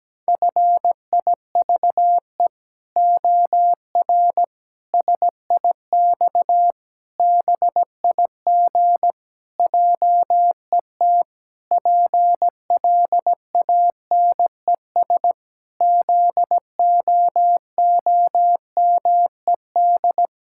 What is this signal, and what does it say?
Morse code, heard as a keyed tone, about 17 wpm, 700 Hz. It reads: FIVE OR SIX BIG JET PLANES ZOOMED
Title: Pangram Challenge v2 - 17wpm